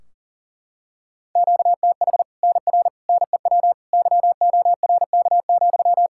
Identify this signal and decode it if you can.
Morse code, heard as a keyed tone, about 40 wpm, 700 Hz. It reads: QTH NR NEW YORK,
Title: Practice QSO - Exchange 1 - Beginning - 40wpm